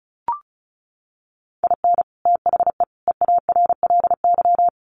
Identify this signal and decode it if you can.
Morse code, heard as a keyed tone, about 34 wpm, 700 Hz. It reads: IN THE EARLY
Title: Sets of 3 Words 35wpm